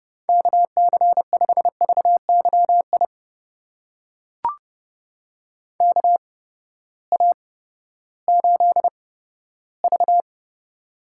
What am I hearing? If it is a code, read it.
Morse code, heard as a keyed tone, about 32 wpm, spaced as for 15 wpm, 700 Hz. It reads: KC5VYI KA8V
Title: USA Call Signs - 6 Characters - 8x Character Spacing - 30wpm